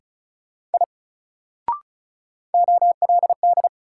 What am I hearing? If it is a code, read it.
Morse code, heard as a keyed tone, about 36 wpm, 700 Hz. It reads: I OLD
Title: Sentences from Top 200 Words - New Word Drill - 35wpm